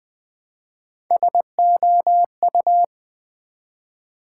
Morse code, heard as a keyed tone, about 20 wpm, 700 Hz. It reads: SOU
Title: Build A Word - 1 Letter At A Time - 3 to 5-Letter Words - 20wpm